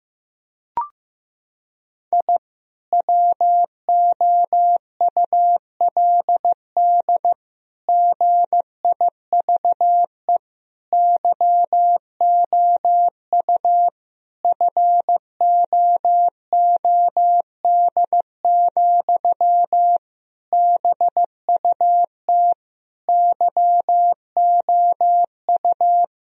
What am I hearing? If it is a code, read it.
Morse code, heard as a keyed tone, about 15 wpm, 700 Hz. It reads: I WOULD GIVE YOU FOOD, BUT YOU
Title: Sentences from Top 1000 Words - Review Part 1 - 15wpm